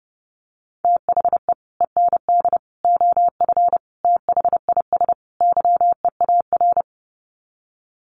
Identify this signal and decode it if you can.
Morse code, heard as a keyed tone, about 30 wpm, 700 Hz. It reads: THE END OF THIS YEAR
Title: Sets of 5 Words Encore 30wpm